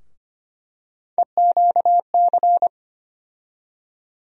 Morse code, heard as a keyed tone, about 25 wpm, 700 Hz. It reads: EQC
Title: USA Call Sign Postfixes - 3 Letters 25wpm